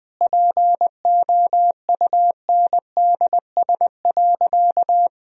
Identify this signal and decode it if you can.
Morse code, heard as a keyed tone, about 20 wpm, 700 Hz. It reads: POUNDS.
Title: Sentences from Top 1000 Words - Review Part 4 - 20wpm